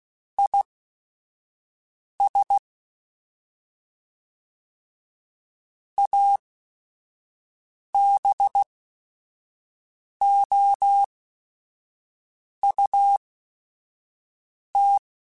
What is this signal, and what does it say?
Morse code, heard as a keyed tone, about 16 wpm, spaced as for 5 wpm, 800 Hz. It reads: IS ABOUT
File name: Weather Sets / Weather Sets-mp3s / Weather 4 21.@5